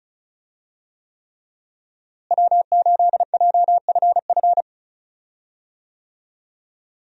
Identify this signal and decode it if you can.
Morse code, heard as a keyed tone, about 34 wpm, 700 Hz. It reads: W8JFF
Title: USA Call Signs - 5 Characters - Repeated 3x - 35wpm